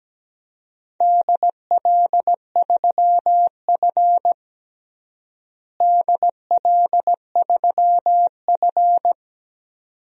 Morse code, heard as a keyed tone, about 17 wpm, 700 Hz. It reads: DL3F DL3F
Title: DXCC-100 Call Signs - 4 Characters - Repeated 2x - 17wpm